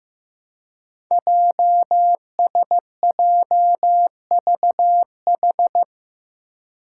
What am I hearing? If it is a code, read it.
Morse code, heard as a keyed tone, about 15 wpm, 700 Hz. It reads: JSJVH
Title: Five Challenging Letter-Number Combinations 15wpm